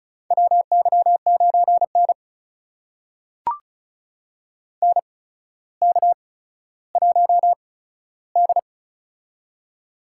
Morse code, heard as a keyed tone, about 36 wpm, spaced as for 15 wpm, 700 Hz. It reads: WY9N NK1D
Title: USA Call Signs - 4 Characters - 8x Character Spacing - 35wpm